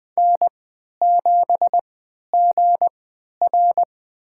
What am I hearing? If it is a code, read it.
Morse code, heard as a keyed tone, about 21 wpm, spaced as for 11 wpm, 700 Hz. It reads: N7GR
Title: USA Call Signs - 3x Character Spacing - 20wpm